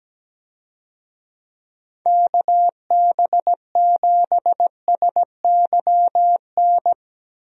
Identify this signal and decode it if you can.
Morse code, heard as a keyed tone, about 17 wpm, 700 Hz. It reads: KB7SYN
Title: USA Call Signs - 6 Characters - 17wpm